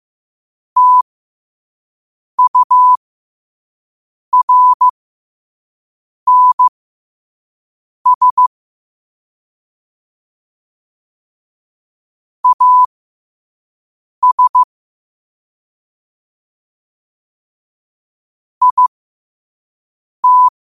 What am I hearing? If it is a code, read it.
Morse code, heard as a keyed tone, about 15 wpm, spaced as for 5 wpm, 1000 Hz. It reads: TURNS AS IT
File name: qotdfiles / 5wpm